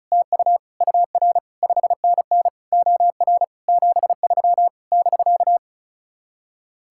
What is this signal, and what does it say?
Morse code, heard as a keyed tone, about 44 wpm, 700 Hz. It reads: TU UR 5NN OR 73 <BK>